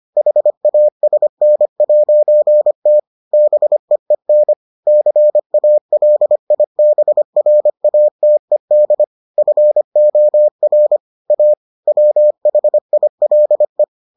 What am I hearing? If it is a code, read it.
Morse code, heard as a keyed tone, about 25 wpm, 600 Hz. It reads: HASN'T BEEN CALIBRATED FOR A WHILE